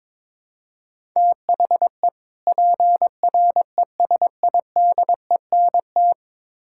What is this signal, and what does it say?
Morse code, heard as a keyed tone, about 22 wpm, 700 Hz. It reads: THE PRESIDENT
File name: Build A Phrase - 2-Word Phrases - 22wpm